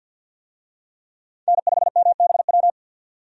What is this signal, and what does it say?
Morse code, heard as a keyed tone, about 48 wpm, 700 Hz. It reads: N5MBW
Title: USA Call Signs - 5 Characters - 50wpm